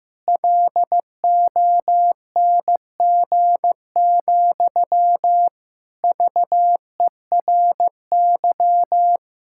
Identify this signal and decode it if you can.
Morse code, heard as a keyed tone, about 15 wpm, 700 Hz. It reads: LONG, VERY
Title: Sentences from Top 1600 Words 15wpm